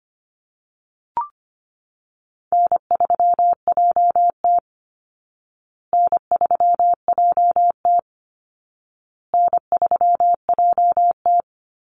Morse code, heard as a keyed tone, about 25 wpm, 700 Hz. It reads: N3JT N3JT N3JT
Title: Easy Call Signs Repeated 3x - 25wpm